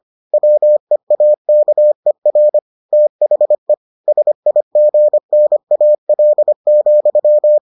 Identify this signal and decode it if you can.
Morse code, heard as a keyed tone, about 25 wpm, 600 Hz. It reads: WEAKER THE SIGNAL,